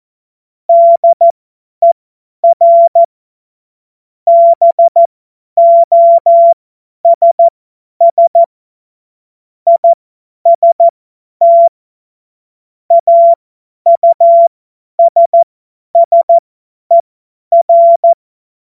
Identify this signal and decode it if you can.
Morse code, heard as a keyed tone, about 14 wpm, spaced as for 10 wpm, 680 Hz. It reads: DER BOSS IST AUSSER
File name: Morsekurs / Witze / Witz 06. Dezember 2022 50 F70